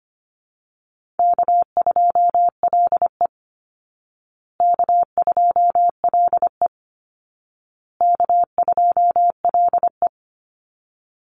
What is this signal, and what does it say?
Morse code, heard as a keyed tone, about 25 wpm, 700 Hz. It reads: K2LE K2LE K2LE
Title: Easy Call Signs Repeated 3x - 25wpm